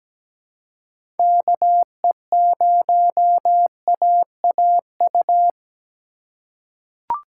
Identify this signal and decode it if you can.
Morse code, heard as a keyed tone, about 17 wpm, 700 Hz. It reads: KE0AAU
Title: USA Call Signs - 2x Character Spacing - 17wpm